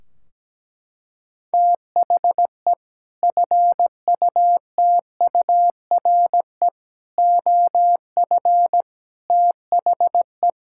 Morse code, heard as a keyed tone, about 17 wpm, 700 Hz. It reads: THE FUTURE OF THE
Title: Sentences from Top 1600 Words 17wpm